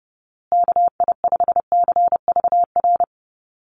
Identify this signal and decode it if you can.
Morse code, heard as a keyed tone, about 30 wpm, 700 Hz. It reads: KI5CVR
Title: USA Call Signs Repeated 2x 30wpm